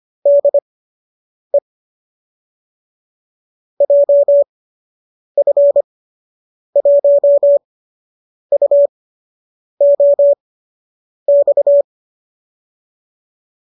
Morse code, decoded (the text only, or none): DE JF1UOX